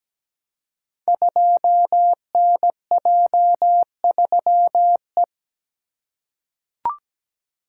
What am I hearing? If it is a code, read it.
Morse code, heard as a keyed tone, about 17 wpm, 700 Hz. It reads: 2NJ3E
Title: Code Groups - 5 Letter:Numbers - 17wpm